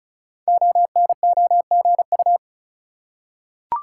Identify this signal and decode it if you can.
Morse code, heard as a keyed tone, about 36 wpm, 700 Hz. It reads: ONOGU
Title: Code Groups - 5 Letters - 35wpm